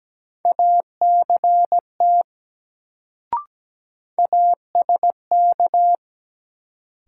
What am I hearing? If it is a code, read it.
Morse code, heard as a keyed tone, about 17 wpm, 700 Hz. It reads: ACT ASK
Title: Sentences from Top 200 Words - New Word Drill - 17wpm